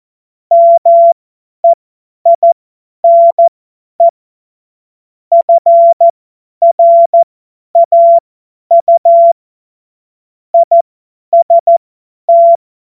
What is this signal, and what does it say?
Morse code, heard as a keyed tone, about 14 wpm, spaced as for 10 wpm, 680 Hz. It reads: MEINE FRAU IST
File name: Morsekurs / Witze / Witz 22. Februar 2022 50 F70